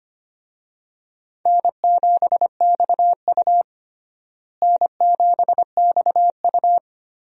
Morse code, heard as a keyed tone, about 25 wpm, 700 Hz. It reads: N7XU N7XU
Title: USA Call Signs - 4 Characters - Repeated 3x - 25wpm